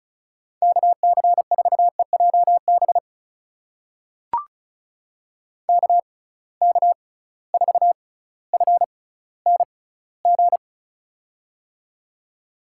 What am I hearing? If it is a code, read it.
Morse code, heard as a keyed tone, about 36 wpm, spaced as for 18 wpm, 700 Hz. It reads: KC4EJB KK4FNG
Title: USA Call Signs - 6 Characters - 6x Character Spacing - 35wpm